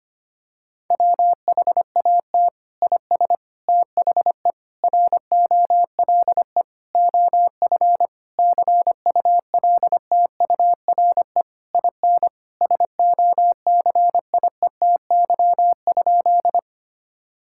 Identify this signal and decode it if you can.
Morse code, heard as a keyed tone, about 25 wpm, 700 Hz. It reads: WHAT IS THE ROLE OF CULTURE IN SOCIETY?
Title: Sentences from Top 1000 Words - Review Part 6 - 25wpm